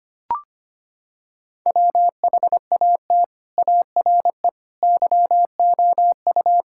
Morse code, heard as a keyed tone, about 25 wpm, 700 Hz. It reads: WHAT ARE YOU
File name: practice / 25wpm / Sentences with 3-Word Phrases - 25wpm